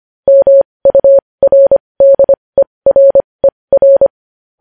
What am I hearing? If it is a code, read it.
Morse code, heard as a keyed tone, about 25 wpm, spaced as for 20 wpm, 550 Hz. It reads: MURDERER
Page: News Headlines in Morse Code at 20 WPM